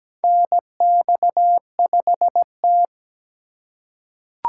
Morse code, heard as a keyed tone, about 17 wpm, 700 Hz. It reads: NX5T
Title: USA Call Signs - 4 Characters - 4x Character Spacing - 17wpm